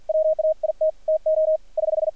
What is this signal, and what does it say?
Morse code, heard as a keyed tone, about 53 wpm, 620 Hz, with radio noise. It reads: WAIT TO <HH>
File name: archived / CRSnet